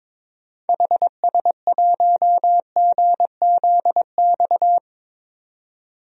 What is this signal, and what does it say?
Morse code, heard as a keyed tone, about 22 wpm, 700 Hz. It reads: HS1GZX
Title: DXCC-100 Call Signs - 6 Characters - 22wpm